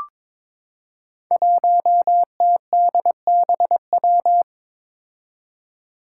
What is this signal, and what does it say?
Morse code, heard as a keyed tone, about 23 wpm, 700 Hz. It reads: 1TDBW
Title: Five Challenging Letter-Number Combinations 22wpm